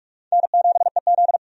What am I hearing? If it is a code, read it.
Morse code, heard as a keyed tone, about 46 wpm, 700 Hz. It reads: N7EZ
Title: USA Call Signs - 4 Characters - 45wpm